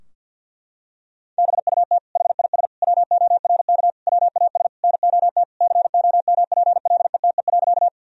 Morse code, heard as a keyed tone, about 50 wpm, 700 Hz. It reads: BUT HIS WORK WAS NOT COMPLETE.
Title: Sentences from Top 1000 Words - Review Part 2 - 50wpm